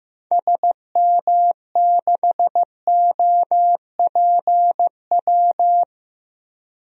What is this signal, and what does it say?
Morse code, heard as a keyed tone, about 16 wpm, 700 Hz. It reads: SM6OPW
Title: DXCC-100 Call Signs - 6 Characters - 15wpm